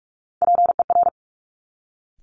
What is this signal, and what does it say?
Morse code, heard as a keyed tone, about 46 wpm, 700 Hz. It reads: PER
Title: Top Three Letter Combinations in Words 45wpm